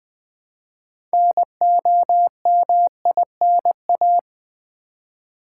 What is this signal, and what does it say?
Morse code, heard as a keyed tone, about 20 wpm, 700 Hz. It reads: NOMINA
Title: Build A Word - 2 Letters At A Time - 8-Letter Words - 20wpm